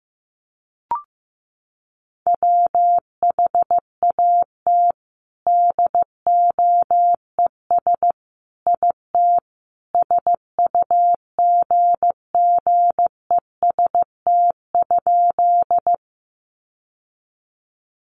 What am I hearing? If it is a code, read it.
Morse code, heard as a keyed tone, about 15 wpm, 700 Hz. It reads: WHAT DOES IT SUGGEST?
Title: Sentences from Top 1300 Words 15wpm